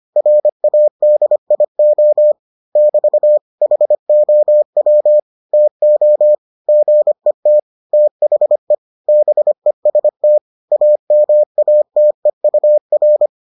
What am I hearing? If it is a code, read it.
Morse code, heard as a keyed tone, about 25 wpm, 600 Hz. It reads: RADIO <BT> HOW TO GET THE BEST AMATEUR